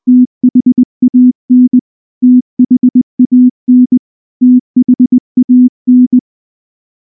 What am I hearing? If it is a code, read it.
Morse code, heard as a keyed tone, about 19 wpm, 250 Hz. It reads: THAN THAN THAN
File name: morse / morse-training-20WPM-8